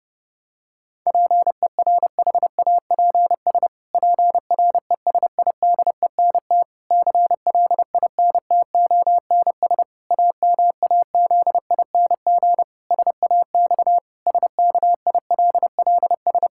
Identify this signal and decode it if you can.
Morse code, heard as a keyed tone, about 32 wpm, 700 Hz. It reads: PERHAPS PRESIDENT CLINTONS AMAZING SAX SKILLS
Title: Pangram Challenge v2 - 30wpm